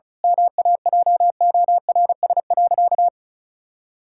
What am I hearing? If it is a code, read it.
Morse code, heard as a keyed tone, about 36 wpm, 700 Hz. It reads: MAJORS.
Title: Pangram Challenge v2 - 35wpm